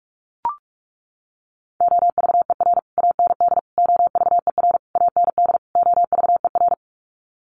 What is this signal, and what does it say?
Morse code, heard as a keyed tone, about 44 wpm, 700 Hz. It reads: OVER AND OVER AND OVER
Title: Sets of 5 Words Encore 45wpm